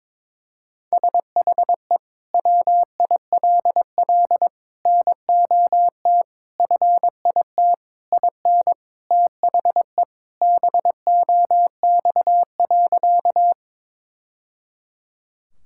Morse code, heard as a keyed tone, about 22 wpm, 700 Hz. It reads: SHE WILL NOT FIT IN THE BOX.
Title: Sentences from Top 1000 Words - Review Part 5 - 22wpm